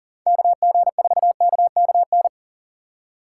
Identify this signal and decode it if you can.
Morse code, heard as a keyed tone, about 40 wpm, 700 Hz. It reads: KG4KKN
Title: USA Call Signs - 6 Characters - 2x Character Spacing - 40wpm